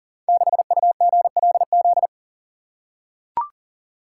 Morse code, heard as a keyed tone, about 42 wpm, 700 Hz. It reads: 6UGLZ
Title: Code Groups - 5 Letter:Numbers - 40wpm